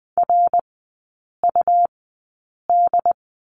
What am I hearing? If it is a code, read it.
Morse code, heard as a keyed tone, about 20 wpm, spaced as for 8 wpm, 700 Hz. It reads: RUD
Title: 100 Lesson 17e - Letter u - Three Characters 20wpm